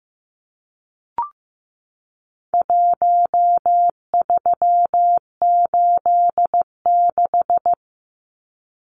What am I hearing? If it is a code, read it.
Morse code, heard as a keyed tone, about 15 wpm, 700 Hz. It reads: 1386
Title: Numbers - Contest Style - 4-Digit 15wpm